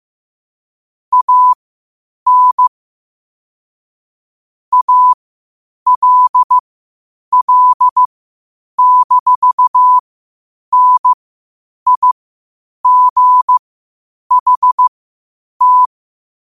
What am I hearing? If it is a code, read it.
Morse code, heard as a keyed tone, about 15 wpm, spaced as for 8 wpm, 1000 Hz. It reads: AN ALL-NIGHT
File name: 8wpm